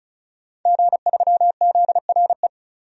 Morse code, heard as a keyed tone, about 34 wpm, 700 Hz. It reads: G3ZRE
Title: DXCC-100 Call Signs - 5 Characters - 35wpm